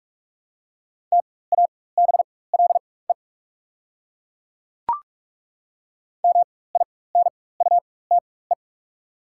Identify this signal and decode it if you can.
Morse code, heard as a keyed tone, about 42 wpm, spaced as for 20 wpm, 690 Hz. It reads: TABLE MINUTE
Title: 4x Character Spacing - Top 500 Words - 45wpm